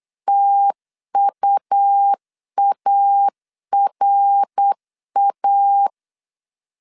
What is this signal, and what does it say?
Morse code, heard as a keyed tone, about 8 wpm, 800 Hz. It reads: TUARA